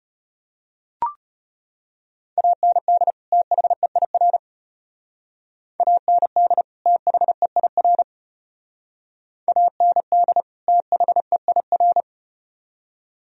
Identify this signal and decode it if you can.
Morse code, heard as a keyed tone, about 35 wpm, 700 Hz. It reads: AND THEIR AND THEIR AND THEIR